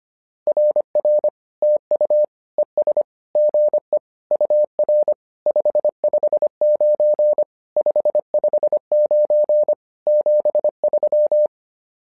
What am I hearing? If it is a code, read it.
Morse code, heard as a keyed tone, about 25 wpm, 600 Hz. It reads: RR TU ES GE UR 559 559 73
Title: SOTA QSO Practice 25wpm